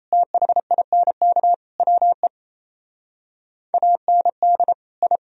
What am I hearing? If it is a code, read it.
Morse code, heard as a keyed tone, about 34 wpm, 700 Hz. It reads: THINK WE AND I
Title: Speed-Racing - Sets of 4 Words 28wpm